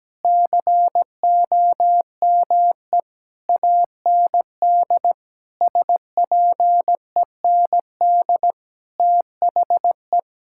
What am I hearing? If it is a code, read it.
Morse code, heard as a keyed tone, about 17 wpm, 700 Hz. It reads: COME AND SPEND THE
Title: Sentences from Top 1000 Words - Review Part 5 - 17wpm